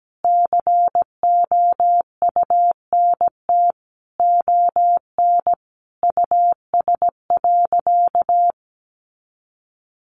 Morse code, heard as a keyed tone, about 17 wpm, 700 Hz. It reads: COUNT ON US.
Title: Sentences from Top 1500 Words 17wpm